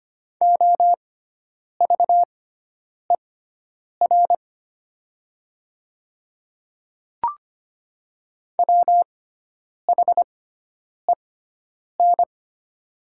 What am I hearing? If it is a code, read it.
Morse code, heard as a keyed tone, about 25 wpm, spaced as for 9 wpm, 700 Hz. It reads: OVER WHEN
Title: 6x Character Spacing - Top 100 Words - 25wpm